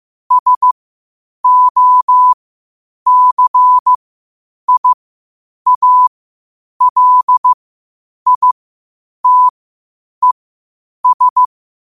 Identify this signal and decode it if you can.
Morse code, heard as a keyed tone, about 15 wpm, spaced as for 8 wpm, 1000 Hz. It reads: SOCIALITES
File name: qotdfiles / 8wpm